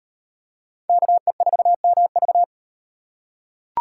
Morse code, heard as a keyed tone, about 38 wpm, 700 Hz. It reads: KE4MV